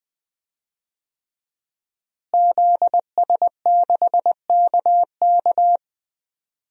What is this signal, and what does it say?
Morse code, heard as a keyed tone, about 20 wpm, 700 Hz. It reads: ZS6KK
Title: DXCC-100 Call Signs - 5 Characters - Repeated 3x - 20wpm